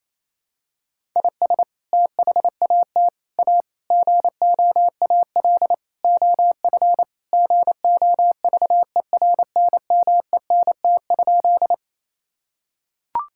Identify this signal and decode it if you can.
Morse code, heard as a keyed tone, about 28 wpm, 700 Hz. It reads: IS THAT A GOAL OF GOVERNMENT?
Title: Sentences from Top 1000 Words - Review Part 5 - 28wpm